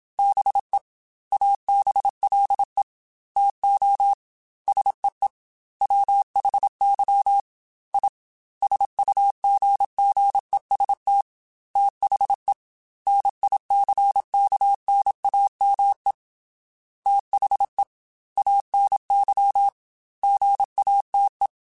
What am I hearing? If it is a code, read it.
Morse code, heard as a keyed tone, about 27 wpm, 800 Hz. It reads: BE ABLE TO SEE WHY I SUGGEST THE NICKNAME THE ANY GATE